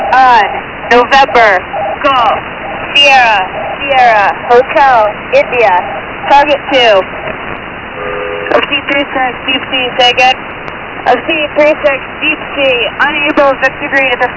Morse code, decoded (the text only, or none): none